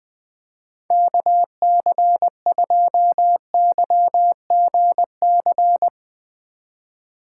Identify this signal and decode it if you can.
Morse code, heard as a keyed tone, about 20 wpm, 700 Hz. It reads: KC2YGC
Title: USA Call Signs - 4x Character Spacing - 20wpm